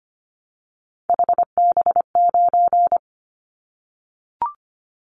Morse code, heard as a keyed tone, about 25 wpm, 700 Hz. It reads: HB9
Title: Callsign Prefixes for Top 25 DXCC - 25wpm